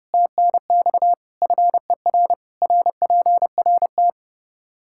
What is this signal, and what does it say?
Morse code, heard as a keyed tone, about 32 wpm, 700 Hz. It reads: TNX FER RPRT
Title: Practice QSO - Exchange 2 - Info - 30wpm